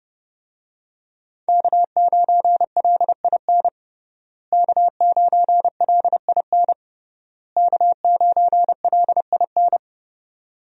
Morse code, heard as a keyed tone, about 32 wpm, 700 Hz. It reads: K9LIN K9LIN K9LIN
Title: USA Call Signs - 5 Characters - Repeated 3x - 30wpm